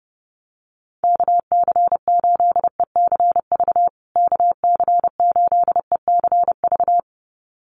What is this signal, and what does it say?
Morse code, heard as a keyed tone, about 30 wpm, 700 Hz. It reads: KC8ECV KC8ECV
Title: USA Call Signs Repeated 2x 30wpm